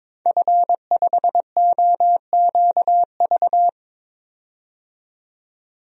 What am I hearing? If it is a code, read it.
Morse code, heard as a keyed tone, about 22 wpm, 700 Hz. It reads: F5OQV
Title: DXCC-100 Call Signs - 5 Characters - 22wpm